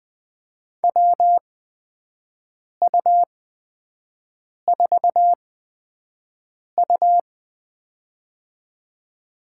Morse code, heard as a keyed tone, about 20 wpm, spaced as for 5 wpm, 700 Hz. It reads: WU4U
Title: USA Call Signs - 4 Characters - 8x Character Spacing - 20wpm